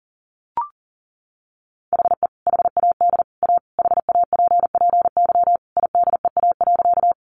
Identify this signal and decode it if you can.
Morse code, heard as a keyed tone, about 38 wpm, 700 Hz. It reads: HE HAD A HAPPY IDEA.
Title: Sentences from Top 1000 Words 40wpm